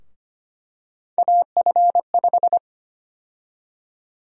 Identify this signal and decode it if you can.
Morse code, heard as a keyed tone, about 25 wpm, 700 Hz. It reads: AF5
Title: USA Call Sign Prefixes - 25wpm